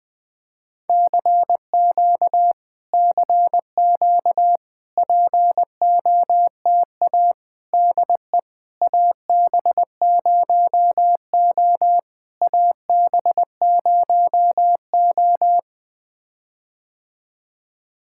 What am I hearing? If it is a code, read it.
Morse code, heard as a keyed tone, about 20 wpm, 700 Hz. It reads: CQ CQ POTA DE AB0O AB0O
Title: POTA QSO Practice - 20wpm